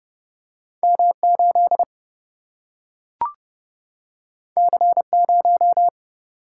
Morse code, heard as a keyed tone, about 30 wpm, 700 Hz. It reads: M 8 C 0
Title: Single-Letter Single-Number 30wpm